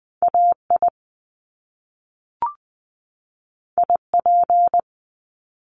Morse code, heard as a keyed tone, about 20 wpm, 700 Hz. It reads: AI IP
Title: Top Two Letter Combinations in Words 20wpm